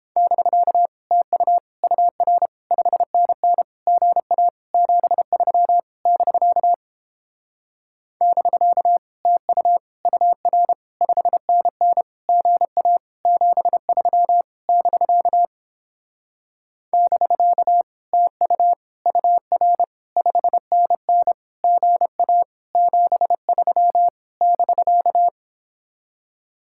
Morse code, heard as a keyed tone, about 29 wpm, 700 Hz. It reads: <BK> TU UR 5NN GA 73 <BK> <BK> TU UR 5NN GA 73 <BK> <BK> TU UR 5NN GA 73 <BK>